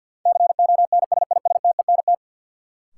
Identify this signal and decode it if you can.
Morse code, heard as a keyed tone, about 53 wpm, 700 Hz. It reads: CONSISTENT
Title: Sentences from Top 1500 Words - New Word Drill - 50wpm